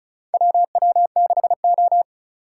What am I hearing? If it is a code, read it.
Morse code, heard as a keyed tone, about 36 wpm, 700 Hz. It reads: WW6O
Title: USA Call Signs - 4 Characters - 3x Character Spacing - 35wpm